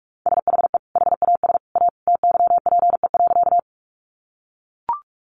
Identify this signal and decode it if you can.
Morse code, heard as a keyed tone, about 44 wpm, 700 Hz. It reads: SHE HAS A TYPE.
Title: Sentences from Top 600 Words 45wpm